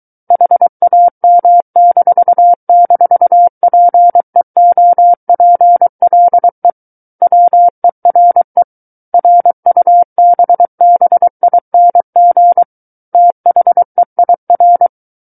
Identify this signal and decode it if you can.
Morse code, heard as a keyed tone, about 23 wpm, 700 Hz. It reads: HAM--PEOPLE WERE RUBBING THEIR
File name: Book One / 14-Chapter 14 - 23 WPM